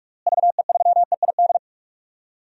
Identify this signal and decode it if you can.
Morse code, heard as a keyed tone, about 46 wpm, 700 Hz. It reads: VE3EID
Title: DXCC-100 Call Signs - 6 Characters - Repeated 3x - 45wpm